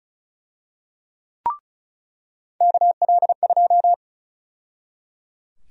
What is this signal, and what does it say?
Morse code, heard as a keyed tone, about 34 wpm, 700 Hz. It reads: KL2
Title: USA Call Sign Prefixes - 35wpm